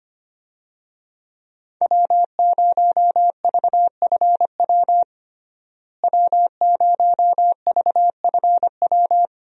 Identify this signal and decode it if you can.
Morse code, heard as a keyed tone, about 25 wpm, 700 Hz. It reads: W0VFW W0VFW
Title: USA Call Signs - 5 Characters - Repeated 2x - 25wpm